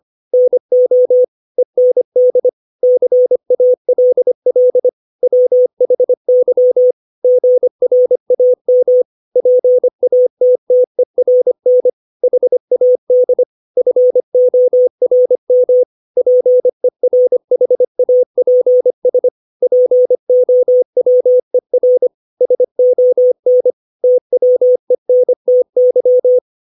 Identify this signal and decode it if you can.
Morse code, heard as a keyed tone, about 25 wpm, 500 Hz. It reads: NO END CALL WHY GRAM PATTERN HAD FORM PERHAPS POWER SON TWENTY
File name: 632 words / 632 Words 25WPM